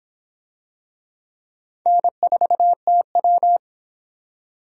Morse code, heard as a keyed tone, about 23 wpm, 700 Hz. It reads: N4TW